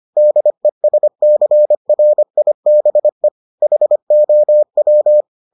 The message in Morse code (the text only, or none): DESCRIBE HOW